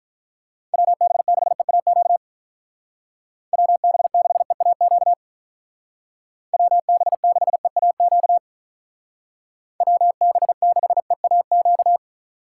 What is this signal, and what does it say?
Morse code, heard as a keyed tone, about 46 wpm, 700 Hz. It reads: WB6EAQ WB6EAQ WB6EAQ WB6EAQ